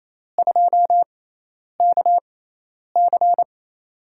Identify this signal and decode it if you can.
Morse code, heard as a keyed tone, about 28 wpm, spaced as for 10 wpm, 700 Hz. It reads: 2KC
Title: USA Call Signs - 5 Characters - 6x Character Spacing - 28wpm